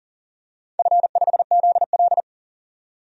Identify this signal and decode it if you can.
Morse code, heard as a keyed tone, about 40 wpm, 700 Hz. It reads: F5ZL
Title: DXCC-100 Call Signs - 4 Characters - 40wpm